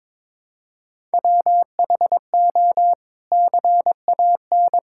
Morse code, heard as a keyed tone, about 22 wpm, 700 Hz. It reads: WHO CAN
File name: practice / 22wpm / Sets of 8 Words - 22wpm